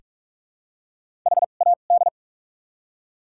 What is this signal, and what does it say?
Morse code, heard as a keyed tone, about 43 wpm, spaced as for 31 wpm, 700 Hz. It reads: HAD
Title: 2x Character Spacing - Top 300 Words - 45wpm